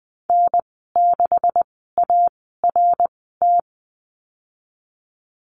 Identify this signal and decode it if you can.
Morse code, heard as a keyed tone, about 20 wpm, spaced as for 15 wpm, 700 Hz. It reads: N6ART
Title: 154 Lesson 25g - Letter p - Call Signs 20wpm